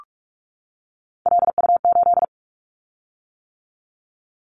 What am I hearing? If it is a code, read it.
Morse code, heard as a keyed tone, about 44 wpm, 700 Hz. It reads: LU8
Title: Callsign Prefixes for Top 100 DXCC - 45wpm